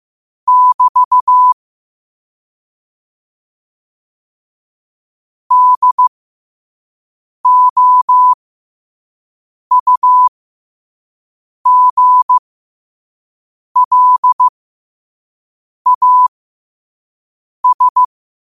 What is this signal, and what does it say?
Morse code, heard as a keyed tone, about 15 wpm, spaced as for 5 wpm, 1000 Hz. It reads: <BT> DOUGLAS
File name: qotdfiles / 5wpm